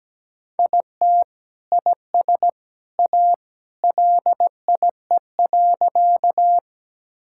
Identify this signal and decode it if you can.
Morse code, heard as a keyed tone, about 17 wpm, 700 Hz. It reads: IT IS A LIE.
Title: Sentences from Top 1600 Words 17wpm